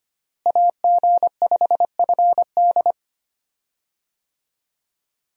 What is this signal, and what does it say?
Morse code, heard as a keyed tone, about 25 wpm, 700 Hz. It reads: AG5FD
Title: USA Call Signs - 5 Characters - Repeated 2x - 25wpm